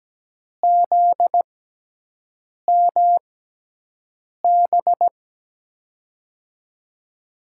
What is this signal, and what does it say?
Morse code, heard as a keyed tone, about 17 wpm, spaced as for 6 wpm, 700 Hz. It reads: ZMB
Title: USA Call Signs - 6 Characters - 6x Character Spacing - 17wpm